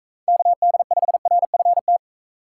Morse code, heard as a keyed tone, about 42 wpm, 700 Hz. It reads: KD5RFT